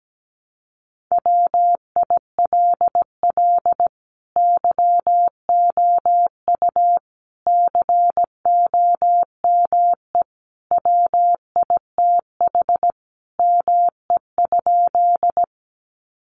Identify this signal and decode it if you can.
Morse code, heard as a keyed tone, about 17 wpm, 700 Hz. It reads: WILL YOU COME WITH ME?
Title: Sentences from Top 200 Words 17wpm